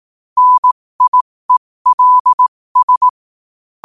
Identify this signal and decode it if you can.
Morse code, heard as a keyed tone, about 18 wpm, spaced as for 15 wpm, 1000 Hz. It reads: NIELS
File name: qotdfiles / 15wpm